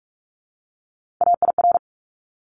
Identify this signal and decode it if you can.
Morse code, heard as a keyed tone, about 44 wpm, 700 Hz. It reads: AIR
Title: Top 200 Words 45wpm